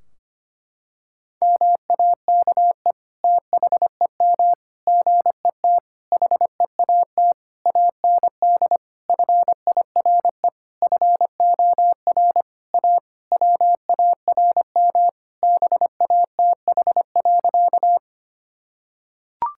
Morse code, heard as a keyed tone, about 25 wpm, 700 Hz. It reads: MAKE THEM GET HEAT AND FIRE FOR A WARM BATH.
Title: Sentences from Top 1600 Words 25wpm